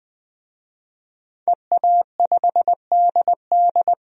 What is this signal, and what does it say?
Morse code, heard as a keyed tone, about 20 wpm, 700 Hz. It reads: EA5DD
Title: DXCC-100 Call Signs - 5 Characters - 20wpm